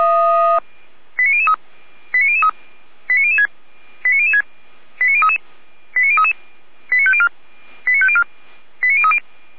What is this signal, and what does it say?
Five-tone selective call (ZVEI), heard as quick runs of five tones, tones of about 70 ms. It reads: 89003 89003 89006 89006 89033 89033 89504 89504 89030